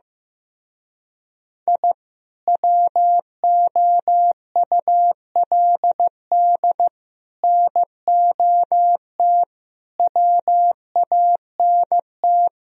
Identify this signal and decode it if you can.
Morse code, heard as a keyed tone, about 15 wpm, 700 Hz. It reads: I WOULD NOT WANT
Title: Speed-Racing - Sets of 5 Words 15wpm